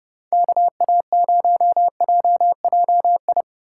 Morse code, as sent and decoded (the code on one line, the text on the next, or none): -.- .- ----- .--- .--- ..
KA0JJI